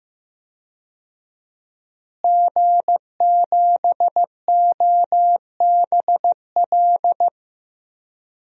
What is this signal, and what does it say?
Morse code, heard as a keyed tone, about 16 wpm, 700 Hz. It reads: G7OBL
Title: DXCC-100 Call Signs - 5 Characters - Repeated 3x - 15wpm